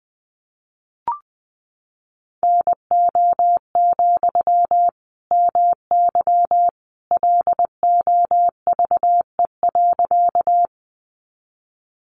Morse code, heard as a keyed tone, about 20 wpm, 700 Hz. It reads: NO, MY LOVE.
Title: Sentences from Top 400 Words 20wpm